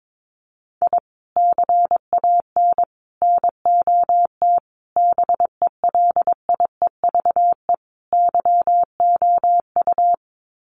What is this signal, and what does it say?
Morse code, heard as a keyed tone, about 22 wpm, 700 Hz. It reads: I CAN NOT BELIEVE YOU
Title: Sets of 5 Words Encore 22wpm